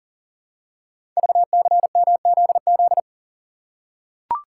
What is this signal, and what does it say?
Morse code, heard as a keyed tone, about 41 wpm, 700 Hz. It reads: VCMZZ
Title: Code Groups - 5 Letters - 40wpm